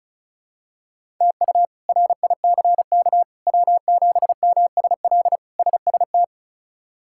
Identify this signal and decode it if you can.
Morse code, heard as a keyed tone, about 34 wpm, 700 Hz. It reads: TU RICK W7MSL SST
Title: K1USN SST QSO Practice - 35wpm